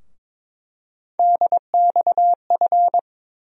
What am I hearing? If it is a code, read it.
Morse code, heard as a keyed tone, about 22 wpm, 700 Hz. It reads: DXF
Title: USA Call Sign Postfixes - 3 Letters 22wpm